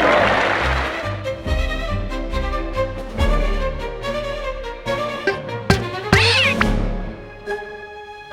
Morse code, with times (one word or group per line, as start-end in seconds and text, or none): none